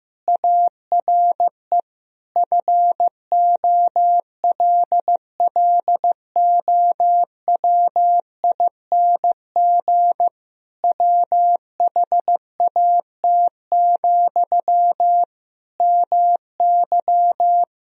Morse code, heard as a keyed tone, about 15 wpm, 700 Hz. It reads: ARE FOLLOWING WHAT, MY
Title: Sentences from Top 1000 Words - Review Part 4 - 15wpm